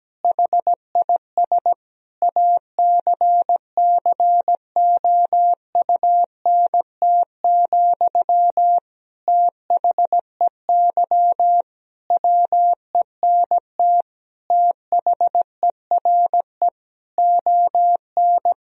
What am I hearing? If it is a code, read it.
Morse code, heard as a keyed tone, about 17 wpm, 700 Hz. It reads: HIS ACCOUNT, THEY WENT THERE ON
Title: Sentences from Top 1000 Words - Review Part 4 - 17wpm